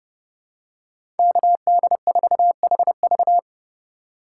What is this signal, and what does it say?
Morse code, heard as a keyed tone, about 31 wpm, 700 Hz. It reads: KD4HV
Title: USA Call Signs - 5 Characters - 8x Character Spacing - 30wpm